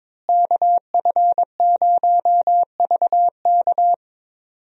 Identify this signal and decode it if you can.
Morse code, heard as a keyed tone, about 22 wpm, 700 Hz. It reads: KF0VK
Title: USA Call Signs - 5 Characters - Repeated 3x - 22wpm